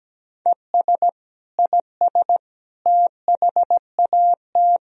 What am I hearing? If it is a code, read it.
Morse code, heard as a keyed tone, about 17 wpm, 700 Hz. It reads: ES IS THAT
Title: Sentences from Top 100 Words and Abbreviations 17wpm